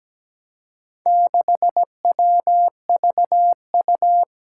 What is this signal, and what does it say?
Morse code, heard as a keyed tone, about 17 wpm, 700 Hz. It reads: 6WVU
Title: Four Challenging Letter-Number Combinations 17wpm